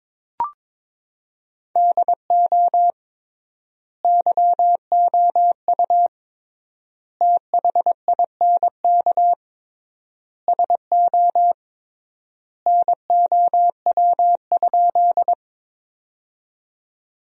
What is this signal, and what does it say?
Morse code, heard as a keyed tone, about 22 wpm, 700 Hz. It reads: DO YOU THINK SO NOW?
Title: 3x Word-Spacing - Sentences from Top 200 Words - 22wpm